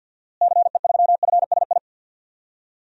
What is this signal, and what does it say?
Morse code, heard as a keyed tone, about 48 wpm, 700 Hz. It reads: CE3FSI
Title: DXCC-100 Call Signs - 6 Characters - Repeated 2x - 50wpm